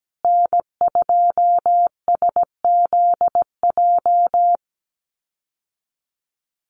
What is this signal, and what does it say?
Morse code, heard as a keyed tone, about 17 wpm, 700 Hz. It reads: N2SZJ
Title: USA Call Signs Repeated 2x 17wpm